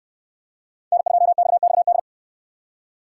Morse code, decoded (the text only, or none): N2XXD